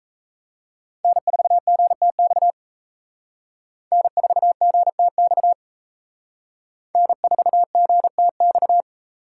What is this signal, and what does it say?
Morse code, heard as a keyed tone, about 35 wpm, 700 Hz. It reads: N4GTX N4GTX N4GTX
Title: Speed Racing - USA 1x3 Call Signs 28wpm